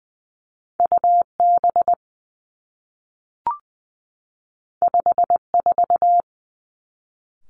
Morse code, heard as a keyed tone, about 20 wpm, 700 Hz. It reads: U B 5 4
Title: Two Challenging Letter-Number Combinations 20wpm